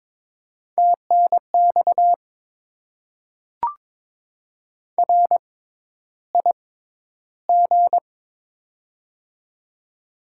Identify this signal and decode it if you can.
Morse code, heard as a keyed tone, about 22 wpm, spaced as for 11 wpm, 700 Hz. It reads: TNX RIG
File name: Top 20 QSO Elements - 6x Character Spacing - 22wpm